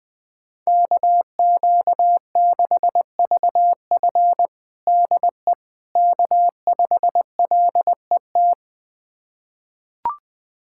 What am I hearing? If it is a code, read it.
Morse code, heard as a keyed tone, about 21 wpm, 700 Hz. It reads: KQ6VF DE K5LET
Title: USA Callsign DE Callsign - 5 Characters - 20wpm